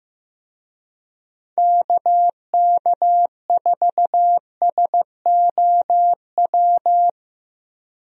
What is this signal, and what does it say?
Morse code, heard as a keyed tone, about 15 wpm, 700 Hz. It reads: KK4SOW
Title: USA Call Signs - 6 Characters - Repeated 2x - 15wpm